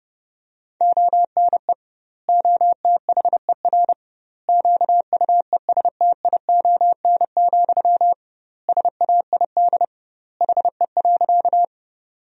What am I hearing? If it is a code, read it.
Morse code, heard as a keyed tone, about 32 wpm, 700 Hz. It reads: ONE OTHER QUESTION, SAID HE.
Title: 2x Word-Spacing - Sentences from Top 400 Words - 30wpm